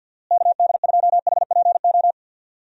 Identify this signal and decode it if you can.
Morse code, heard as a keyed tone, about 51 wpm, 700 Hz. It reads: KD2HPO
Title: USA Call Signs - 6 Characters - Repeated 2x - 50wpm